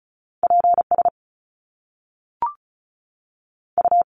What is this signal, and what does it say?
Morse code, heard as a keyed tone, about 36 wpm, 700 Hz. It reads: PS U
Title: Top Two Letter Combinations in Words 35wpm